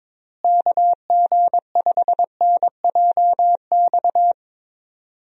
Morse code, heard as a keyed tone, about 22 wpm, 700 Hz. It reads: KG5NJX
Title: USA Call Signs - 6 Characters - Repeated 2x - 22wpm